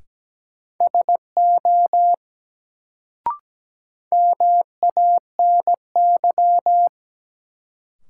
Morse code, heard as a keyed tone, about 17 wpm, 700 Hz. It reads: SO MANY
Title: Sets of 2 Words - New Word Drill - 17wpm